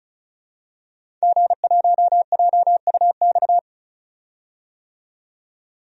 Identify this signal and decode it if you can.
Morse code, heard as a keyed tone, about 34 wpm, 700 Hz. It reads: G1JUX
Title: DXCC-100 Call Signs - 5 Characters - 35wpm